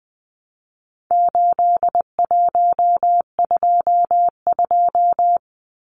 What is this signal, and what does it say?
Morse code, heard as a keyed tone, about 20 wpm, 700 Hz. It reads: 8122
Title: Numbers - 4-Digit 20wpm